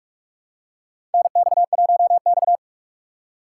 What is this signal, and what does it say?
Morse code, heard as a keyed tone, about 46 wpm, 700 Hz. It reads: NX1X
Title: USA Call Signs - 4x Character Spacing - 45wpm